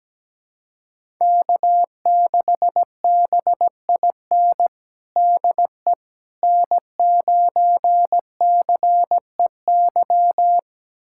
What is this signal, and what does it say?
Morse code, heard as a keyed tone, about 17 wpm, 700 Hz. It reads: K6BIN DE N9CEY
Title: USA Callsign DE Callsign - 5 Characters - 17wpm